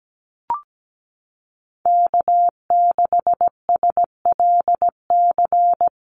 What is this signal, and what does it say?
Morse code, heard as a keyed tone, about 17 wpm, 700 Hz. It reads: K6SLC
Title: USA Call Signs Repeated 2x 17wpm